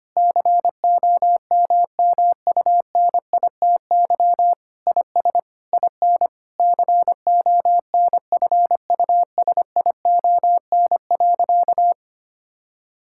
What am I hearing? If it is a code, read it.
Morse code, heard as a keyed tone, about 25 wpm, 700 Hz. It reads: COMMUNITY IS IN CONFUSION.
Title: Sentences from Top 1600 Words 25wpm